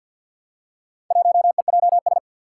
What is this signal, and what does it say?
Morse code, heard as a keyed tone, about 48 wpm, 700 Hz. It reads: 1EJS
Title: Four Challenging Letter-Number Combinations 50wpm